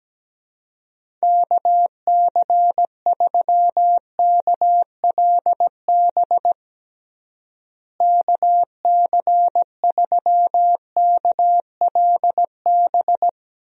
Morse code, heard as a keyed tone, about 17 wpm, 700 Hz. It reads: KC3KLB KC3KLB
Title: USA Call Signs - 6 Characters - Repeated 3x - 17wpm